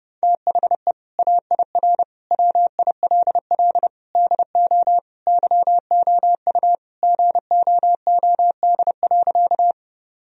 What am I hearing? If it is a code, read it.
Morse code, heard as a keyed tone, about 31 wpm, 700 Hz. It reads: THE AIR WILL DO YOU GOOD.
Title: Sentences from Top 1000 Words - Review Part 1 - 30wpm